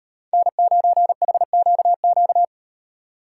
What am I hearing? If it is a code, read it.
Morse code, heard as a keyed tone, about 36 wpm, 700 Hz. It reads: N9HQQ